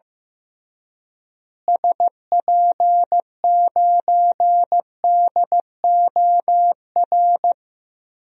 Morse code, heard as a keyed tone, about 15 wpm, 700 Hz. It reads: SP9DOR
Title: DXCC-100 Call Signs - 6 Characters - Repeated 3x - 15wpm